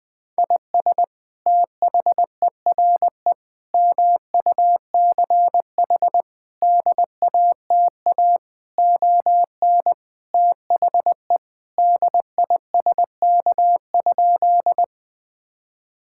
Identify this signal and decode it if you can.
Morse code, heard as a keyed tone, about 20 wpm, 700 Hz. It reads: IS THERE MUCH DATA ON THE DISK?
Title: Sentences from Top 1000 Words - Review Part 6 - 20wpm